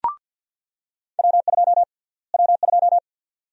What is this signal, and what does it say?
Morse code, heard as a keyed tone, about 48 wpm, 700 Hz. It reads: W2 W2
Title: USA Call Sign Prefixes - Repeated 2x - 50wpm